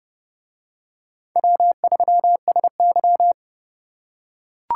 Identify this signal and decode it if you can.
Morse code, heard as a keyed tone, about 26 wpm, 700 Hz. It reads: W3SY